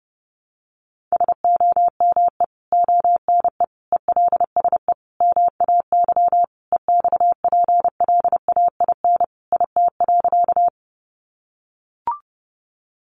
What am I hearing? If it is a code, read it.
Morse code, heard as a keyed tone, about 30 wpm, 700 Hz. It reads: SOME ONE ELSE MAY EXPLAIN IT.
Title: Sentences from Top 1100 Words 30wpm